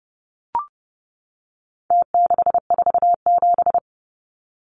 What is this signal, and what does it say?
Morse code, heard as a keyed tone, about 30 wpm, 700 Hz. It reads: T647
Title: Numbers - Contest Style - 4-Digit 30wpm